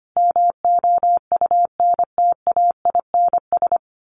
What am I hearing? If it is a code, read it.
Morse code, heard as a keyed tone, about 25 wpm, 700 Hz. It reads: MOUNTAINS
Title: A Sound Of Thunder 25wpm